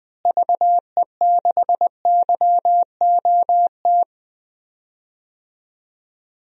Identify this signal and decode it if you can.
Morse code, heard as a keyed tone, about 20 wpm, 700 Hz. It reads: VE6YOT
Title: DXCC-100 Call Signs - 6 Characters - Repeated 3x - 20wpm